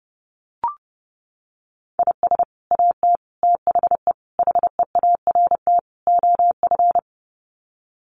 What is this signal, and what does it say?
Morse code, heard as a keyed tone, about 30 wpm, 700 Hz. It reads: IS AT THE HEART OF
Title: Sets of 5 Words Encore2 30wpm